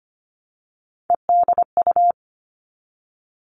Morse code, heard as a keyed tone, about 25 wpm, 700 Hz. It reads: EDU
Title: Three Challenging Letter-Number Combinations 25wpm